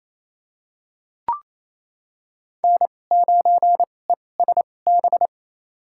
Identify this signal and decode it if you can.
Morse code, heard as a keyed tone, about 28 wpm, spaced as for 20 wpm, 700 Hz. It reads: N9ESB
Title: USA Call Signs - 2x Character Spacing - 28wpm